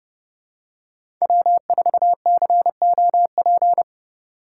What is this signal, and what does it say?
Morse code, heard as a keyed tone, about 32 wpm, 700 Hz. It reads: W4COP
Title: USA Call Signs - 5 Characters - 2x Character Spacing - 30wpm